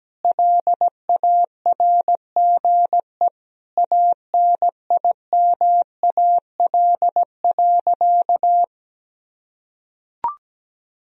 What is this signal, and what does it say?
Morse code, heard as a keyed tone, about 17 wpm, 700 Hz. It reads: LARGE ANIMAL.
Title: Sentences from Top 1000 Words - Review Part 1 - 17wpm